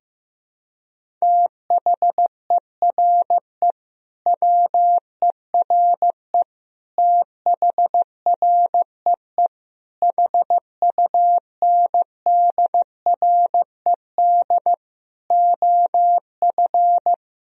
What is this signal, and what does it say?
Morse code, heard as a keyed tone, about 15 wpm, 700 Hz. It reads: THERE WERE THREE HUNDRED OF
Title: Sentences from Top 1000 Words - Review Part 2 - 15wpm